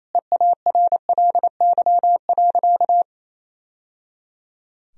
Morse code, heard as a keyed tone, about 28 wpm, 700 Hz. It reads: EARLY.
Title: Sentences from Top 1000 Words - Review Part 4 - 28wpm